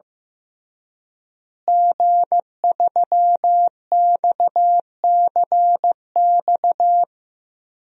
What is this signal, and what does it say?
Morse code, heard as a keyed tone, about 14 wpm, 700 Hz. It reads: G3XCX
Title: DXCC-100 Call Signs - 5 Characters - Repeated 3x - 15wpm